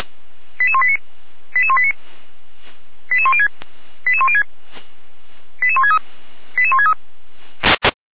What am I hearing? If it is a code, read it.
Five-tone selective call (ZVEI), heard as quick runs of five tones, tones of about 80 ms. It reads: 88189 88189 88106 88106 88163 88163
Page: SDR/ZVEI